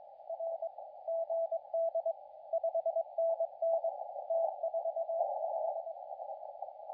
Morse code, heard as a keyed tone, about 21 wpm, 660 Hz, with radio noise. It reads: R GD 5NN T45
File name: la4c / sac-cw-2017 / CONTEST AUDIO 2100Z